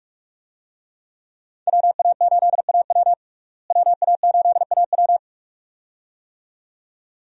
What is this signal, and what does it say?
Morse code, heard as a keyed tone, about 41 wpm, 700 Hz. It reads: WA8AW WA8AW
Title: USA Call Signs - 5 Characters - Repeated 2x - 45wpm